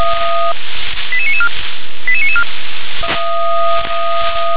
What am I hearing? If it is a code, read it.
Five-tone selective call (ZVEI), heard as quick runs of five tones, tones of about 70 ms. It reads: 88994 88994
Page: SDR/ZVEI